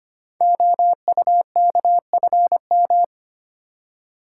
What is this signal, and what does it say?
Morse code, heard as a keyed tone, about 25 wpm, 700 Hz. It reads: OUKFM
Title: Code Groups - 5 Letters - 25wpm